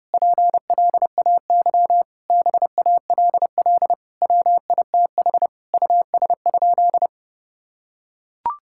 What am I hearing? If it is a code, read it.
Morse code, heard as a keyed tone, about 30 wpm, 700 Hz. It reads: PLAY BALL WITH US?
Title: Sentences from Top 1000 Words - Review Part 6 - 30wpm